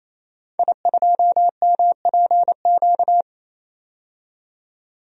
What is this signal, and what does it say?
Morse code, heard as a keyed tone, about 27 wpm, 700 Hz. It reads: I2MPQ
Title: DXCC-100 Call Signs - 5 Characters - 28wpm